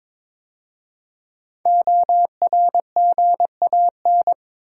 Morse code, heard as a keyed tone, about 22 wpm, 690 Hz. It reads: ORGAN
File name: 5 Letter Words - First 3 Letters Given - 22wpm